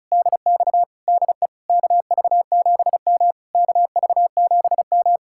Morse code, heard as a keyed tone, about 45 wpm, 700 Hz. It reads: DX DE KV7M KV7M